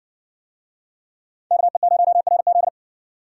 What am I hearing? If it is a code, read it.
Morse code, heard as a keyed tone, about 58 wpm, 700 Hz. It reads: KE0RZ